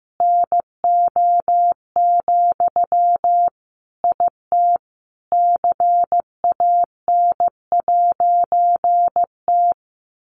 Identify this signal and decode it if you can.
Morse code, heard as a keyed tone, about 15 wpm, 700 Hz. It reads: NO, IT CAN'T
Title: A Sound Of Thunder 15wpm